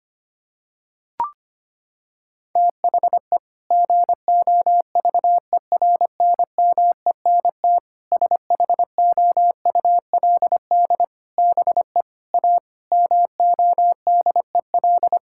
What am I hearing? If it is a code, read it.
Morse code, heard as a keyed tone, about 25 wpm, 700 Hz. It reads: THE GOVERNMENT SHOULD BE A MODEL
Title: Sentences from Top 1600 Words 25wpm